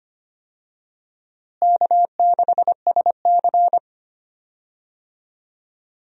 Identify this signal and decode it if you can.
Morse code, heard as a keyed tone, about 25 wpm, 700 Hz. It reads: K6SC
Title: USA Call Signs - 4 Characters - Repeated 2x - 25wpm